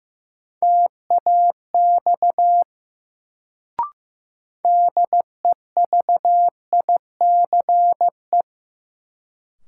Morse code, heard as a keyed tone, about 15 wpm, 700 Hz. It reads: TAX DEVICE
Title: Sentences from Top 1500 Words - New Word Drill Alt - 15wpm